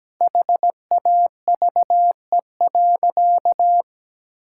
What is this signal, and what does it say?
Morse code, heard as a keyed tone, about 17 wpm, 700 Hz. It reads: HAVE.
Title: Sentences from Top 1000 Words - Review Part 6 - 17wpm